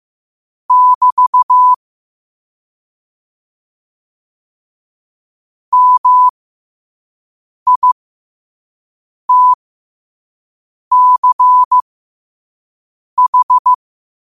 Morse code, heard as a keyed tone, about 15 wpm, spaced as for 5 wpm, 1000 Hz. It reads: <BT> MITCH
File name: qotdfiles / 5wpm